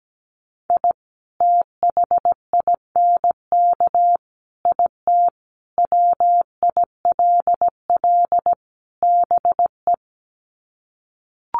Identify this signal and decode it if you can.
Morse code, heard as a keyed tone, about 17 wpm, 700 Hz. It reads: I THINK IT WILL BE
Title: Sets of 5 Words Encore2 17wpm